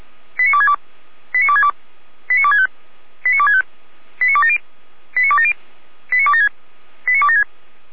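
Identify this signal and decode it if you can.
Five-tone selective call (ZVEI), heard as quick runs of five tones, tones of about 70 ms. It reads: 89262 89262 89265 89265 89270 89270 89276 89276
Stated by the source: ZVEI five-tone selective call 89280 received by SDR.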